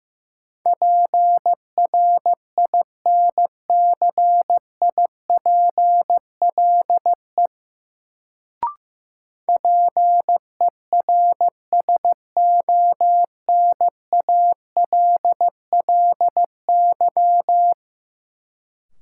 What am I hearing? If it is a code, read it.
Morse code, heard as a keyed tone, about 15 wpm, 700 Hz. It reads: PRINCIPLE PERSONALLY
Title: Sentences from Top 1400 Words - New Word Drill - 15wpm